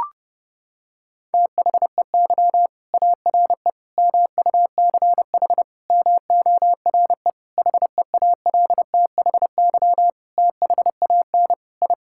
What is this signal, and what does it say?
Morse code, heard as a keyed tone, about 30 wpm, 700 Hz. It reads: THEY ARE MUCH MORE HEALTHY THAN I
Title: Sentences from Top 1000 Words - Review Part 5 - 30wpm